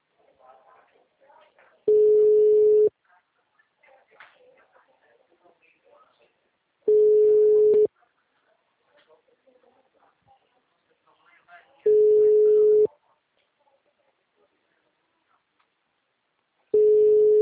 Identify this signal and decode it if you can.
Ringback tone (Europe, 425 Hz), with speech.
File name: out- -2005- - - .81163